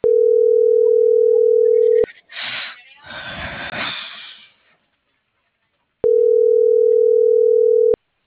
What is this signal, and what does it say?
Ringback tone (North America), 440 + 480 Hz.
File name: CENTRAL BLN / out ericka.canales - .75995